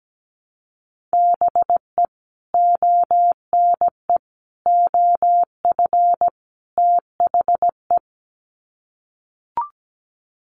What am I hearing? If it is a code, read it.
Morse code, heard as a keyed tone, about 17 wpm, 700 Hz. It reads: BE ONE OF THE
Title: Sets of 4 Words 17wpm